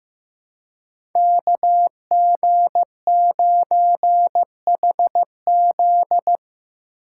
Morse code, heard as a keyed tone, about 15 wpm, 700 Hz. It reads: KG9HZ
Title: USA Call Signs - 5 Characters - 2x Character Spacing - 15wpm